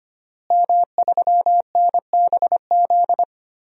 Morse code, heard as a keyed tone, about 25 wpm, 700 Hz. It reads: M3NBZ
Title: DXCC-100 Call Signs - 5 Characters - Repeated 2x - 25wpm